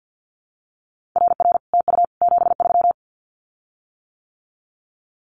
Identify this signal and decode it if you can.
Morse code, heard as a keyed tone, about 50 wpm, 700 Hz. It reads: RR TU 73
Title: Common QSO Phrases Encore 50wpm